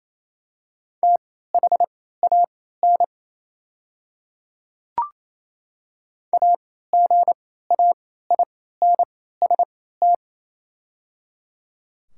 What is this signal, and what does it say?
Morse code, heard as a keyed tone, about 28 wpm, spaced as for 16 wpm, 700 Hz. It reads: THAN AGAINST
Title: 3x Character Spacing - Top 500 Words - 28wpm